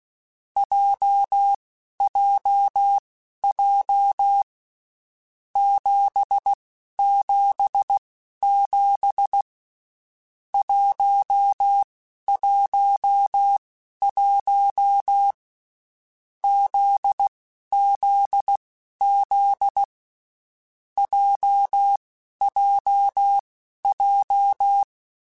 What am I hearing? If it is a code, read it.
Morse code, heard as a keyed tone, about 16 wpm, spaced as for 11 wpm, 800 Hz. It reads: JJJ 777 111 ZZZ JJJ
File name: Basic Sets / Basic Letter Groups Collective7 / JZ17@10